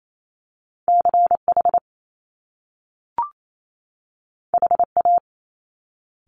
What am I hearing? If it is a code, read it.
Morse code, heard as a keyed tone, about 28 wpm, 700 Hz. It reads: CH HA
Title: Top Two Letter Combinations in Words 28wpm